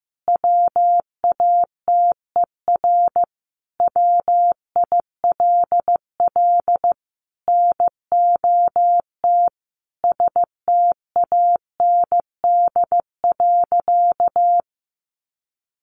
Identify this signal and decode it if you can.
Morse code, heard as a keyed tone, about 15 wpm, 700 Hz. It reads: WATER WILL NOT STAND.
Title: Sentences from Top 1300 Words 15wpm